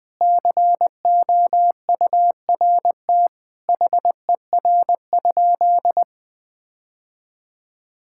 Morse code, heard as a keyed tone, about 20 wpm, 700 Hz. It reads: COURT HER?
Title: Sentences from Top 1600 Words 20wpm